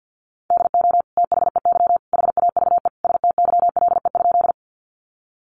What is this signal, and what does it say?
Morse code, heard as a keyed tone, about 50 wpm, 700 Hz. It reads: DO THEY HAVE STYLE?
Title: Sentences from Top 800 Words 50wpm